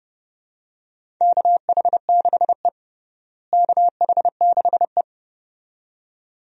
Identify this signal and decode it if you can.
Morse code, heard as a keyed tone, about 30 wpm, 700 Hz. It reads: KH6E KH6E
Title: DXCC-100 Call Signs - 4 Characters - Repeated 2x - 30wpm